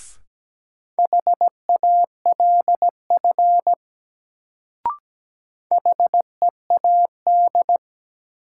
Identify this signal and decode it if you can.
Morse code, heard as a keyed tone, about 17 wpm, 700 Hz. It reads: HALF HEAD
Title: Sets of 3 Words - New Word Drill - 17wpm